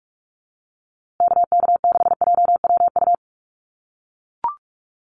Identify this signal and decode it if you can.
Morse code, heard as a keyed tone, about 44 wpm, 700 Hz. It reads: KK6JWU
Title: USA Call Signs Repeated 3x 45wpm